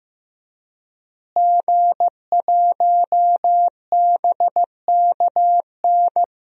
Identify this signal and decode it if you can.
Morse code, heard as a keyed tone, about 15 wpm, 700 Hz. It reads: G1BKN
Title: DXCC-100 Call Signs - 5 Characters - Repeated 3x - 15wpm